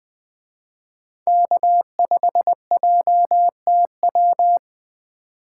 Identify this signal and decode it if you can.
Morse code, heard as a keyed tone, about 20 wpm, 700 Hz. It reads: K5JTW
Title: Speed Racing - USA 1x3 Call Signs 17wpm